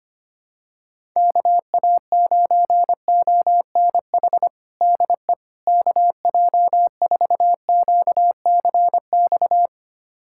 Morse code, heard as a keyed tone, about 25 wpm, 700 Hz. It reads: KA9ONH DE KJ4QCX
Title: USA Callsign DE Callsign - 6 Characters - 25wpm